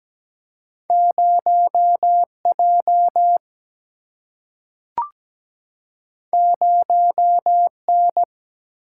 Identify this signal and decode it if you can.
Morse code, heard as a keyed tone, about 17 wpm, 700 Hz. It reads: 0J 0N
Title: Single-Letter Single-Number 17wpm